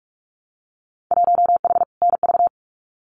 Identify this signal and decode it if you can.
Morse code, heard as a keyed tone, about 46 wpm, 700 Hz. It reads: 1H NV
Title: Winter Field Day Exchanges 45wpm